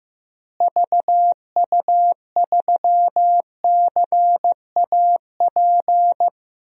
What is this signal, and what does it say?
Morse code, heard as a keyed tone, about 15 wpm, 700 Hz. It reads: VU3CAP
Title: DXCC-100 Call Signs - 6 Characters - Repeated 3x - 15wpm